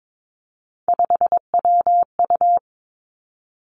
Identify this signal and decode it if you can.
Morse code, heard as a keyed tone, about 22 wpm, 700 Hz. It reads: 5WU
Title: Three Challenging Letter-Number Combinations 22wpm